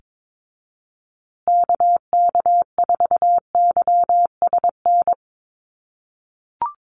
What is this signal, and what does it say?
Morse code, heard as a keyed tone, about 22 wpm, 700 Hz. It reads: KK4YSN
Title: USA Call Signs Repeated 2x 22wpm